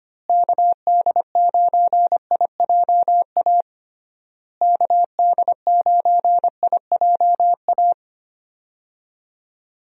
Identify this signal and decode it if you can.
Morse code, heard as a keyed tone, about 25 wpm, 700 Hz. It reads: KD9IJA KD9IJA
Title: USA Call Signs - 6 Characters - Repeated 3x - 25wpm